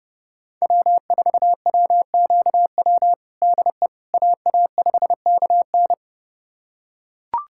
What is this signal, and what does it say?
Morse code, heard as a keyed tone, about 30 wpm, 700 Hz. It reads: W4WQW DE AA5KN
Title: USA Callsign DE Callsign - 5 Characters - 30wpm